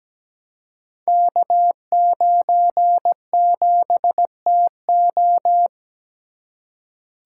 Morse code, heard as a keyed tone, about 17 wpm, 700 Hz. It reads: K97TO
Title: Code Groups - 5 Letter:Numbers - 17wpm